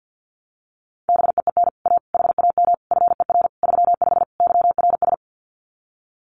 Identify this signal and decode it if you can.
Morse code, heard as a keyed tone, about 50 wpm, 700 Hz. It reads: BEEN A HAM FER 35 YRS
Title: Common QSO Phrases Encore 50wpm